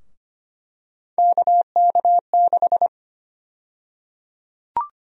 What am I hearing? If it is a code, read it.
Morse code, heard as a keyed tone, about 25 wpm, 700 Hz. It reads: KK6
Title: USA Call Sign Prefixes - 25wpm